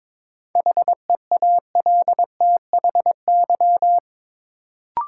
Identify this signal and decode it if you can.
Morse code, heard as a keyed tone, about 23 wpm, 700 Hz. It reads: HEALTHY
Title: Sentences from Top 1000 Words - Review Part 5 - New Word Drill - 22wpm